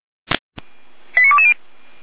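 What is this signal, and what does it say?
Five-tone selective call (ZVEI), heard as quick runs of five tones, tones of about 70 ms. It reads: 89220
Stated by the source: Five-tone ZVEI selective call 89130 received by SDR.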